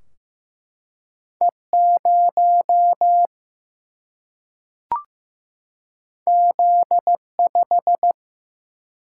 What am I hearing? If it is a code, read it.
Morse code, heard as a keyed tone, about 15 wpm, 700 Hz. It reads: E0 Z5
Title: Single-Letter Single-Number 15wpm